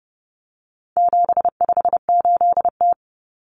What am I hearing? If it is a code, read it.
Morse code, heard as a keyed tone, about 30 wpm, 700 Hz. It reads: 758T
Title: Common Cut Numbers - 4-Digit 30wpm